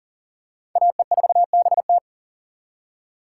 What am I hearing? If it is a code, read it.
Morse code, heard as a keyed tone, about 40 wpm, 700 Hz. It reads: AE4BT
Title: USA Call Signs - 5 Characters - 4x Character Spacing - 40wpm